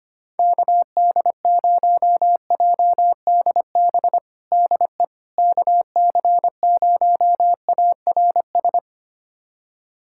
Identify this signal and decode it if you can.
Morse code, heard as a keyed tone, about 25 wpm, 700 Hz. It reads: KD0JDB DE KC0ARS
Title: USA Callsign DE Callsign - 6 Characters - 25wpm